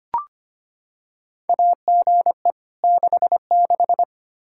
Morse code, heard as a keyed tone, about 25 wpm, 700 Hz. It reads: AGE 66
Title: K1USN SST QSO Practice 25wpm